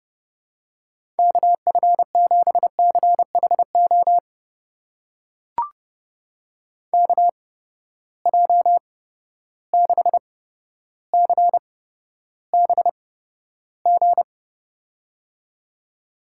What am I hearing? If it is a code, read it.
Morse code, heard as a keyed tone, about 30 wpm, spaced as for 13 wpm, 700 Hz. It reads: KF7CHO KJ6CBG
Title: USA Call Signs - 6 Characters - 8x Character Spacing - 30wpm